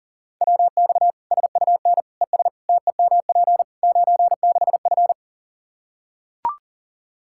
Morse code, heard as a keyed tone, about 40 wpm, 700 Hz. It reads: WX SUN ES TEMP 96F
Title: Practice QSO - Exchange 2 - Info - 40wpm